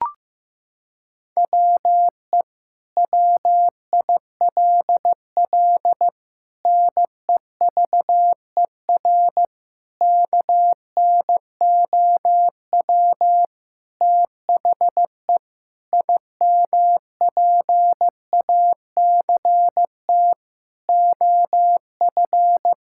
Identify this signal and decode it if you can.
Morse code, heard as a keyed tone, about 15 wpm, 700 Hz. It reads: WE WILL NEVER KNOW THE IMPACT OF
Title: Sentences from Top 1000 Words - Review Part 6 - 15wpm